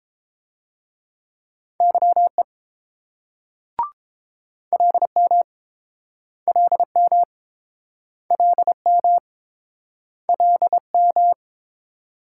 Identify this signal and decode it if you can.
Morse code, heard as a keyed tone, about 31 wpm, 700 Hz. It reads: YE LM LM LM LM